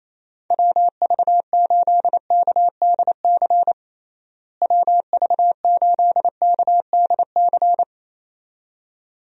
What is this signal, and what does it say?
Morse code, heard as a keyed tone, about 28 wpm, 700 Hz. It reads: WV8KDC WV8KDC
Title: USA Call Signs - 6 Characters - Repeated 2x - 28wpm